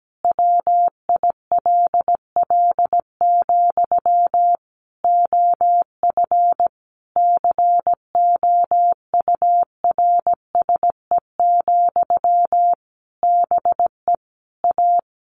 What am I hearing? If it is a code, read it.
Morse code, heard as a keyed tone, about 17 wpm, 700 Hz. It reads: WILL, OF COURSE, BE A
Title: Sentences from Top 1400 Words 17wpm